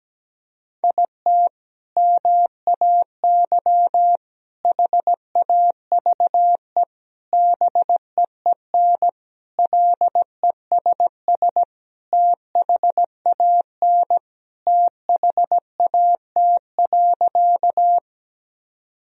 Morse code, heard as a keyed tone, about 17 wpm, 700 Hz. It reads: IT MAY HAVE BEEN LESS THAN THAT.
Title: Sentences from Top 1000 Words - Review Part 2 - 17wpm